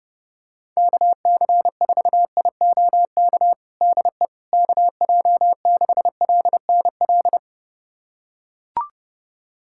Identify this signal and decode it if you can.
Morse code, heard as a keyed tone, about 30 wpm, 700 Hz. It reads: KC4IOK DE KJ6LNL
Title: USA Callsign DE Callsign - 6 Characters - 30wpm